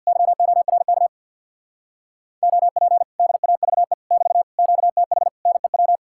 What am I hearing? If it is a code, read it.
Morse code, heard as a keyed tone, about 60 wpm, 700 Hz. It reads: YORK OP DAVE <BT> QTH NEW
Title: Speed-Racing Common QSO Phrases Encore - 50wpm